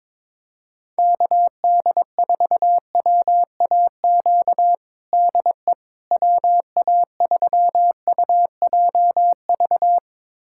Morse code, heard as a keyed tone, about 22 wpm, 700 Hz. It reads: KD4WAQ DE WA3UJV
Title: USA Callsign DE Callsign 22wpm